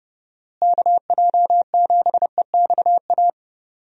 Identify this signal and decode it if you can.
Morse code, heard as a keyed tone, about 32 wpm, 700 Hz. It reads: KJ7EXA
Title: USA Call Signs - 6 Characters - Repeated 2x - 30wpm